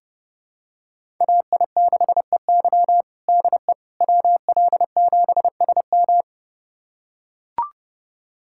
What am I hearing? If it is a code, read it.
Morse code, heard as a keyed tone, about 31 wpm, 700 Hz. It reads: AI6EY DE WL7SM
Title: USA Callsign DE Callsign - 5 Characters - 30wpm